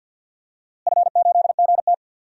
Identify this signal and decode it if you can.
Morse code, heard as a keyed tone, about 52 wpm, 700 Hz. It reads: F8GT
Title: DXCC-100 Call Signs - 4 Characters - Repeated 3x - 50wpm